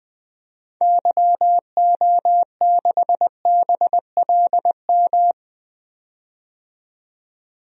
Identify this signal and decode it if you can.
Morse code, heard as a keyed tone, about 21 wpm, 700 Hz. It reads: YO6BLM
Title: DXCC-100 Call Signs - 6 Characters - 20wpm